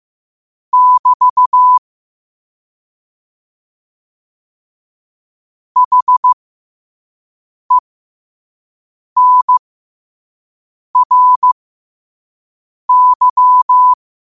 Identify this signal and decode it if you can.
Morse code, heard as a keyed tone, about 15 wpm, spaced as for 5 wpm, 1000 Hz. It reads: <BT> HENRY
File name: qotdfiles / 5wpm